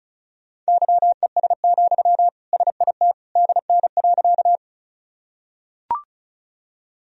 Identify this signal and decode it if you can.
Morse code, heard as a keyed tone, about 36 wpm, 700 Hz. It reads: YES, SIT DN.
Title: Sentences from Top 500 Words and Abbreviations 35wpm